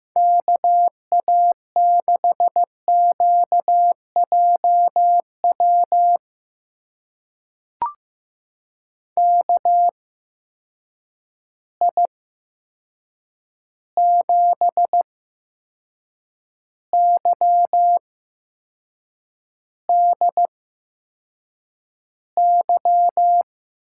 Morse code, heard as a keyed tone, about 15 wpm, spaced as for 6 wpm, 700 Hz. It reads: KA6QJW KI7YDY
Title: USA Call Signs - 6 Characters - 8x Character Spacing - 15wpm